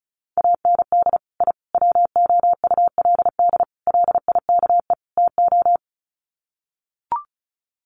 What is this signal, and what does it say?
Morse code, heard as a keyed tone, about 34 wpm, 700 Hz. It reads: AND I WOULD LIKE TO
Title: Sets of 5 Words Encore 35wpm